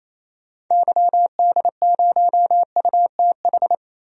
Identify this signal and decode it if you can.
Morse code, heard as a keyed tone, about 28 wpm, 700 Hz. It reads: YD0UTH
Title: DXCC-100 Call Signs - 6 Characters - 28wpm